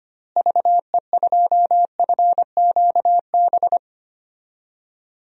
Morse code, heard as a keyed tone, about 25 wpm, 700 Hz. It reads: VE2FQB
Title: DXCC-100 Call Signs - 6 Characters - Repeated 2x - 25wpm